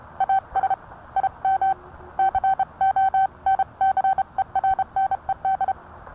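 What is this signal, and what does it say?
Morse code, heard as a keyed tone, about 30 wpm, 770 Hz. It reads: AS IM CONCERNED
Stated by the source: Morse code sent by N1EA on a Vibroplex key.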